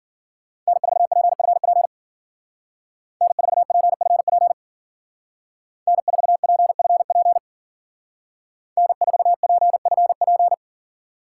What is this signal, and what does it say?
Morse code, heard as a keyed tone, about 51 wpm, 700 Hz. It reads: N4PFP N4PFP N4PFP N4PFP